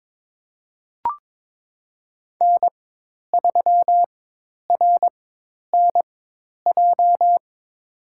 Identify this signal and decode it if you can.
Morse code, heard as a keyed tone, about 22 wpm, spaced as for 10 wpm, 700 Hz. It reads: N3RNJ
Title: USA Call Signs - 5 Characters - 4x Character Spacing - 22wpm